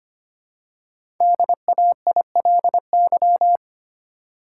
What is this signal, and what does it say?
Morse code, heard as a keyed tone, about 26 wpm, 700 Hz. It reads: DAILY
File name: 5 Letter Words - First 3 Letters Given - 25wpm